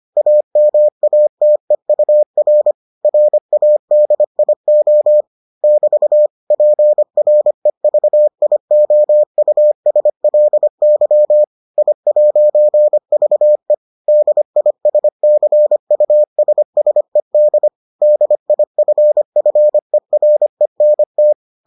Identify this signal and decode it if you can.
Morse code, heard as a keyed tone, about 25 wpm, 600 Hz. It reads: AMATEUR RADIO <BT> PREVIOUSLY I'VE DISCUSSED DIFFERENT